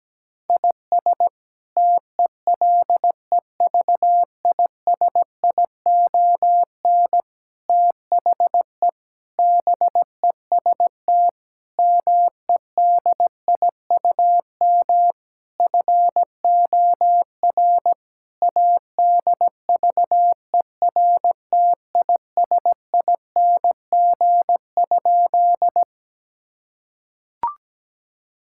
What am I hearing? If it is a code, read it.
Morse code, heard as a keyed tone, about 17 wpm, 700 Hz. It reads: IS TELEVISION THE BEST MEDIUM FOR ADVERTISING?
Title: Sentences from Top 1000 Words - Review Part 6 - 17wpm